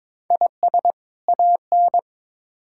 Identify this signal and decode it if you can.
Morse code, heard as a keyed tone, about 21 wpm, 700 Hz. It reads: IS AN
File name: Build A Phrase - 2-Word Phrases - 22wpm